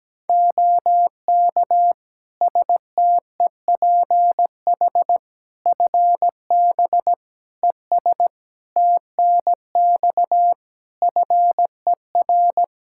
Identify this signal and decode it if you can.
Morse code, heard as a keyed tone, about 17 wpm, 700 Hz. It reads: OK STEPH FB ES TNX FER
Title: Practice QSO - Exchange 2 - Info - 17wpm